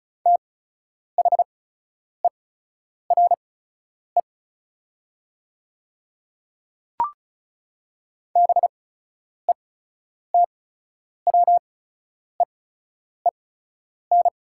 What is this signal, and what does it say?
Morse code, heard as a keyed tone, about 36 wpm, spaced as for 10 wpm, 690 Hz. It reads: THERE BETWEEN
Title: 8x Character Spacing - Top 500 Words - 35wpm